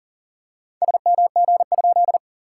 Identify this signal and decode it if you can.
Morse code, heard as a keyed tone, about 38 wpm, 690 Hz. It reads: SMG?
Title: SOTA QSO Practice 40wpm